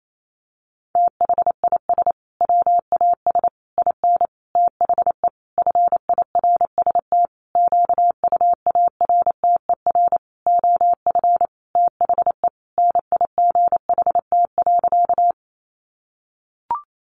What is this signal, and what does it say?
Morse code, heard as a keyed tone, about 28 wpm, 700 Hz. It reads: THIS WAS IN THE FIRST QUARTER OF THE NIGHT.
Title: Sentences from Top 1300 Words 28wpm